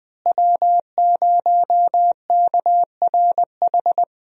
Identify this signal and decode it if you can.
Morse code, heard as a keyed tone, about 21 wpm, 700 Hz. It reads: W0KRH
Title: USA Call Signs - 5 Characters - Repeated 2x - 20wpm